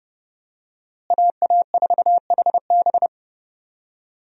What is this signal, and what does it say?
Morse code, heard as a keyed tone, about 30 wpm, 700 Hz. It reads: AA4HB
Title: USA Call Signs - 5 Characters - Repeated 2x - 30wpm